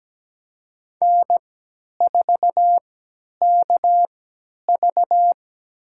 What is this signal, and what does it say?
Morse code, heard as a keyed tone, about 17 wpm, spaced as for 10 wpm, 700 Hz. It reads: N4KV
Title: USA Call Signs - 4 Characters - 3x Character Spacing - 17wpm